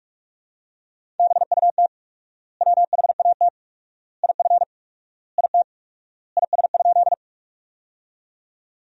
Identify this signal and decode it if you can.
Morse code, heard as a keyed tone, about 46 wpm, 700 Hz. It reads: BUT WHAT IF IT IS?
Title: 4x Word-Spacing - Sentences from Top 100 Words - 45wpm